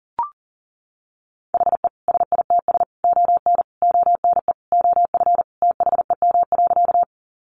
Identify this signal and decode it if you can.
Morse code, heard as a keyed tone, about 37 wpm, 700 Hz. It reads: HE SITS ON ONE OF THEM.
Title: Additional Easy Phrases 40wpm